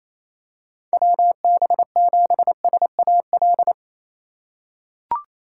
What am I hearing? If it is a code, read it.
Morse code, heard as a keyed tone, about 28 wpm, 700 Hz. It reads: WB7SAL
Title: USA Call Signs - 6 Characters - 4x Character Spacing - 28wpm